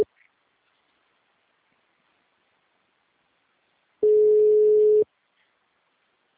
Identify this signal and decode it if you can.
Ringback tone (Europe, 425 Hz).